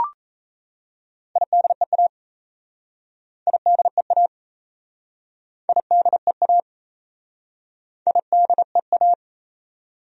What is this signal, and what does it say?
Morse code, heard as a keyed tone, about 37 wpm, 700 Hz. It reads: IDEA IDEA IDEA IDEA